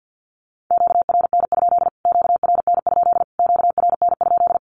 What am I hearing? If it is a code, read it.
Morse code, heard as a keyed tone, about 50 wpm, 700 Hz. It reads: QRN? QRN? QRN?
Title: Q-Codes Repeated 3x 50wpm